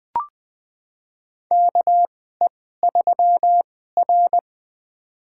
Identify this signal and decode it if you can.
Morse code, heard as a keyed tone, about 20 wpm, spaced as for 14 wpm, 700 Hz. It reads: KE3R
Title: USA Call Signs - 4 Characters - 2x Character Spacing - 20wpm